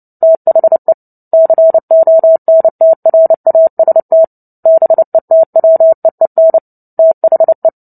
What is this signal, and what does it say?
Morse code, heard as a keyed tone, about 30 wpm, 650 Hz. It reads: THE CONTRAST BETWEEN THE
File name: Book Two / 03-Chapter 3 - 29 WPM